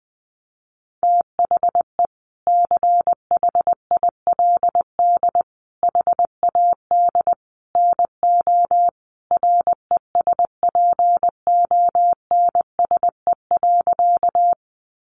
Morse code, heard as a keyed tone, about 20 wpm, 700 Hz. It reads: THE CHILD HAD NO RESPONSE.
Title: Sentences from Top 1400 Words 20wpm